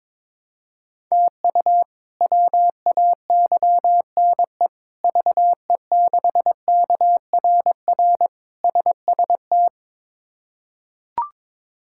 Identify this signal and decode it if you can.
Morse code, heard as a keyed tone, about 21 wpm, 700 Hz. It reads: TU WAYNE VE6KRR SST
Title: K1USN SST QSO Practice - 22wpm